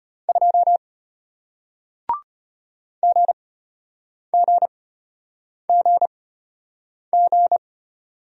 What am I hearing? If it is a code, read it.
Morse code, heard as a keyed tone, about 34 wpm, spaced as for 8 wpm, 700 Hz. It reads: 2 GGGG